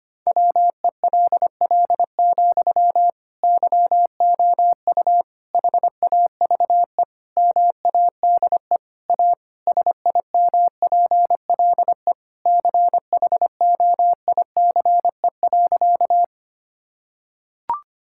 Morse code, heard as a keyed tone, about 25 wpm, 700 Hz. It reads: WELL, YOU HAVE MADE A SIMPLE CHOICE.
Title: Sentences from Top 1000 Words - Review Part 6 - 25wpm